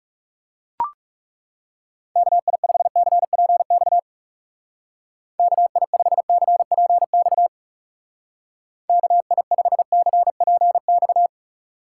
Morse code, heard as a keyed tone, about 35 wpm, 700 Hz. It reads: KI5CPX KI5CPX KI5CPX
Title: Speed Racing - USA Call Signs 30wpm